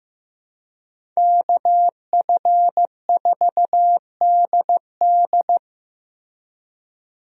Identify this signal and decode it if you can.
Morse code, heard as a keyed tone, about 15 wpm, 700 Hz. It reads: KF4DD
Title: USA Call Signs - 5 Characters - Repeated 3x - 15wpm